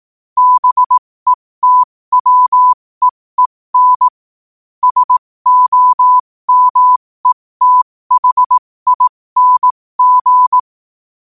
Morse code, heard as a keyed tone, about 18 wpm, spaced as for 15 wpm, 1000 Hz. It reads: BETWEEN SOMETHING
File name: qotdfiles / 15wpm